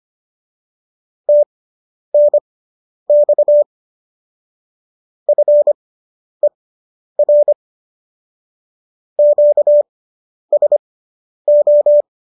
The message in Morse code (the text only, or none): TNX FER QSO